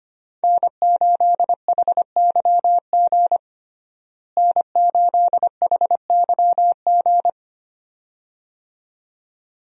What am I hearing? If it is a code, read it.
Morse code, heard as a keyed tone, about 25 wpm, 700 Hz. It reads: N8HYG N8HYG
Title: USA Call Signs - 5 Characters - Repeated 2x - 25wpm